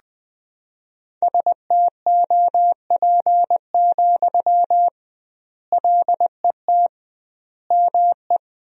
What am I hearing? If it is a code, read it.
Morse code, heard as a keyed tone, about 20 wpm, 700 Hz. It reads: STOP, LET ME
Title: 2x Word-Spacing - Sentences from Top 300 Words - 20wpm